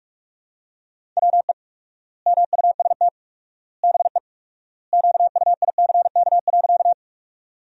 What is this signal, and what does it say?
Morse code, heard as a keyed tone, about 46 wpm, 700 Hz. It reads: WE MUST BE QUICK.
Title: 4x Word-Spacing - Sentences from Top 500 Words - 45wpm